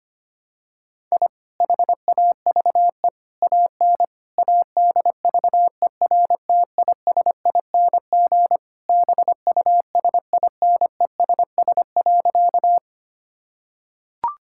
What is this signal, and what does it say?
Morse code, heard as a keyed tone, about 25 wpm, 700 Hz. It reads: I HAVE AN ADVERTISING BUSINESS.
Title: Sentences from Top 1000 Words - Review Part 6 - 25wpm